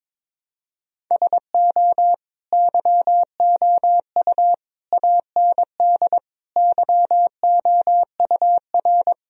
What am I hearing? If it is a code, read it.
Morse code, heard as a keyed tone, about 22 wpm, 700 Hz. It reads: SO YOU AND YOUR
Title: Sentences from Top 1000 Words - Review Part 4 - 22wpm